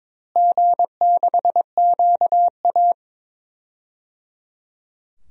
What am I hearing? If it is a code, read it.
Morse code, heard as a keyed tone, about 22 wpm, 700 Hz. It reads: G6QA
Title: DXCC-100 Call Signs - 4 Characters - Repeated 3x - 22wpm